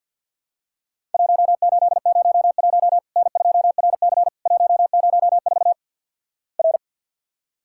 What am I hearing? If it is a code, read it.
Morse code, heard as a keyed tone, about 50 wpm, 700 Hz. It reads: 1801 N2RC 104 R
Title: Worked All Europe DX Contest - QTC Exchange - 50wpm